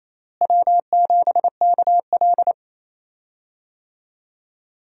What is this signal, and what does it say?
Morse code, heard as a keyed tone, about 28 wpm, 700 Hz. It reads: W7KL
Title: USA Call Signs - 4 Characters - Repeated 2x - 28wpm